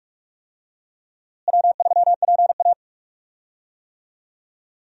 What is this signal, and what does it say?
Morse code, heard as a keyed tone, about 42 wpm, 700 Hz. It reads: W3PA
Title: USA Call Signs - 4 Characters - Repeated 3x - 45wpm